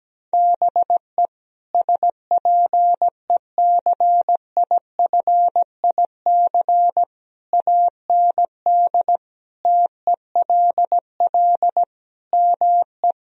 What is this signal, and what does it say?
Morse code, heard as a keyed tone, about 17 wpm, 700 Hz. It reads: BE SPECIFIC AND TELL ME
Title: Sentences from Top 1000 Words - Review Part 3 - 17wpm